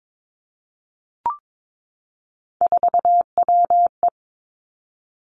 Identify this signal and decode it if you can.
Morse code, heard as a keyed tone, about 22 wpm, 700 Hz. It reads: 4WE
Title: Three Challenging Letter-Number Combinations 22wpm